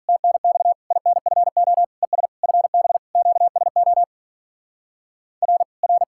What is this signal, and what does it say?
Morse code, heard as a keyed tone, about 48 wpm, 700 Hz. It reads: TNX INFO ES FB QSO R R